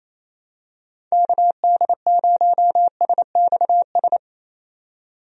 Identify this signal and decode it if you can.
Morse code, heard as a keyed tone, about 27 wpm, 700 Hz. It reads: KD0SXS
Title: USA Call Signs - 6 Characters - 28wpm